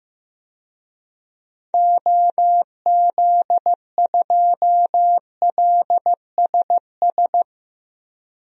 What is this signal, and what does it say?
Morse code, heard as a keyed tone, about 15 wpm, 700 Hz. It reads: OZ2LSS
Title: DXCC-100 Call Signs - 6 Characters - Repeated 3x - 15wpm